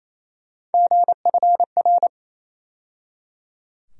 Morse code, heard as a keyed tone, about 28 wpm, 700 Hz. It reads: GFR
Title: USA Call Sign Postfixes - 3 Letters 28wpm